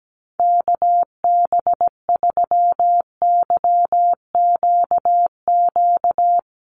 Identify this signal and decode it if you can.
Morse code, heard as a keyed tone, about 17 wpm, 700 Hz. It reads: KB3YQQ
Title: USA Call Signs Repeated 3x 17wpm